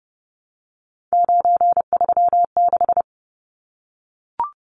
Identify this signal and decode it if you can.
Morse code, heard as a keyed tone, about 30 wpm, 700 Hz. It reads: 936
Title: Numbers - 3-Digit 30wpm